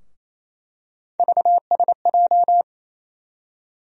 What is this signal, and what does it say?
Morse code, heard as a keyed tone, about 28 wpm, 700 Hz. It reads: VSJ
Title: USA Call Sign Postfixes - 3 Letters 28wpm